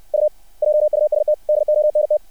Morse code, heard as a keyed tone, about 60 wpm, 600 Hz. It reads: U JUST DONT
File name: archived / CRSnet